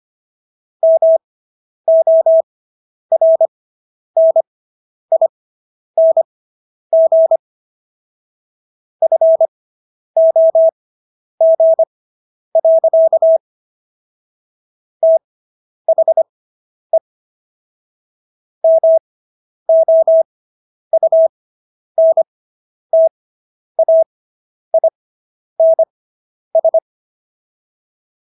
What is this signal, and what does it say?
Morse code, heard as a keyed tone, about 25 wpm, spaced as for 10 wpm, 650 Hz. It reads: MORNING FOG. THE MOUNTAINS